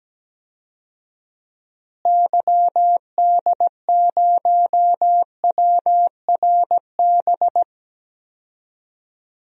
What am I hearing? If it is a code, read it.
Morse code, heard as a keyed tone, about 17 wpm, 700 Hz. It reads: YD0WRB
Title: DXCC-100 Call Signs - 6 Characters - Repeated 3x - 17wpm